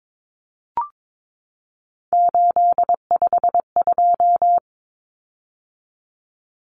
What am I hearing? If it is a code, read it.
Morse code, heard as a keyed tone, about 22 wpm, 700 Hz. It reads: 852
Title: Common Cut Numbers - 3-Digit 22wpm